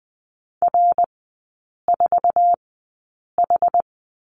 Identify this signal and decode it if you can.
Morse code, heard as a keyed tone, about 20 wpm, spaced as for 8 wpm, 700 Hz. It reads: R4H
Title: 060 Lesson 11e - Letter h - Three Characters 20wpm